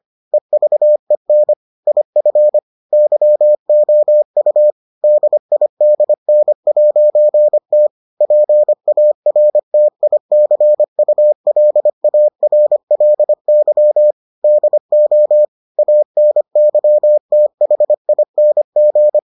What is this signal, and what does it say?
Morse code, heard as a keyed tone, about 25 wpm, 600 Hz. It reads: EVEN IF YOU DIDN'T PARTICULARLY DO ANYTHING